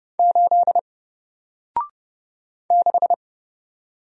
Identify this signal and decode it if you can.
Morse code, heard as a keyed tone, about 30 wpm, 700 Hz. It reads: 8 6
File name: Single-Letter-Number-30wpm